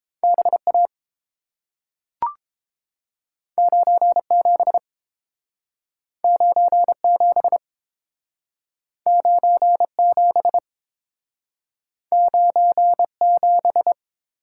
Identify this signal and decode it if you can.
Morse code, heard as a keyed tone, about 30 wpm, 700 Hz. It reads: B A 9 7 9 7 9 7 9 7